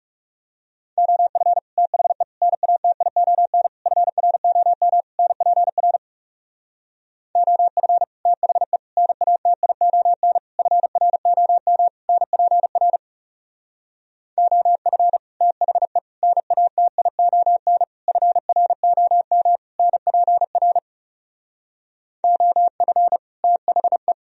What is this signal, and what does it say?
Morse code, heard as a keyed tone, about 37 wpm, 700 Hz. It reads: OF THE NATION FROM NPR OF THE NATION FROM NPR OF THE NATION FROM NPR OF THE
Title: Speed-Racing - Sets of 5 Words 30wpm